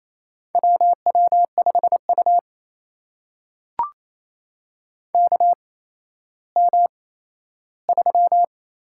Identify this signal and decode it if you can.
Morse code, heard as a keyed tone, about 27 wpm, spaced as for 14 wpm, 700 Hz. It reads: WW5U KM3
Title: USA Call Signs - 4 Characters - 8x Character Spacing - 28wpm